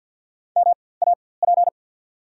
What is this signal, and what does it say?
Morse code, heard as a keyed tone, about 53 wpm, spaced as for 24 wpm, 690 Hz. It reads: MAP
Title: 4x Character Spacing - Top 500 Words - 50wpm